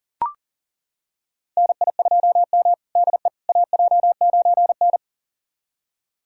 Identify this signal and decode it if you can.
Morse code, heard as a keyed tone, about 37 wpm, 700 Hz. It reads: NI2M DE AJ9N
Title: USA Callsign DE Callsign - 4 Characters - 40wpm